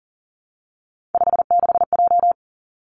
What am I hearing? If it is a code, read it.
Morse code, heard as a keyed tone, about 41 wpm, 700 Hz. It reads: 56J
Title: Three Challenging Letter-Number Combinations 40wpm